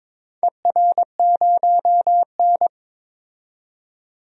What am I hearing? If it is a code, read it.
Morse code, heard as a keyed tone, about 22 wpm, 700 Hz. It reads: ER0N
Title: DXCC-100 Call Signs - 4 Characters - 22wpm